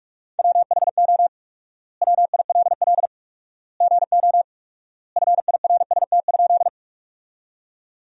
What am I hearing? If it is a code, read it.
Morse code, heard as a keyed tone, about 46 wpm, 700 Hz. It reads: WHO WILL GO FIRST?
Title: 4x Word-Spacing - Sentences from Top 100 Words - 45wpm